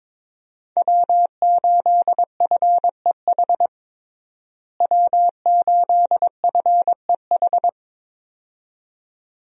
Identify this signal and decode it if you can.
Morse code, heard as a keyed tone, about 22 wpm, 700 Hz. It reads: W8FEH W8FEH
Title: USA Call Signs - 5 Characters - Repeated 2x - 22wpm